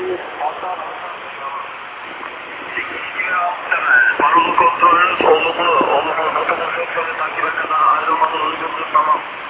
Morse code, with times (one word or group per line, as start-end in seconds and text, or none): none